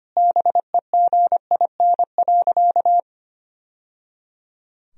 Morse code, heard as a keyed tone, about 25 wpm, 700 Hz. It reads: BEGIN.
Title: Sentences from Top 1000 Words - Review Part 1 - 25wpm